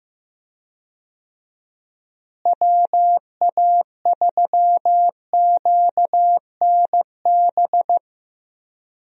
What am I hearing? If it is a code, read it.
Morse code, heard as a keyed tone, about 15 wpm, 700 Hz. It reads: WA3QNB
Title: USA Call Signs - 6 Characters - Repeated 2x - 15wpm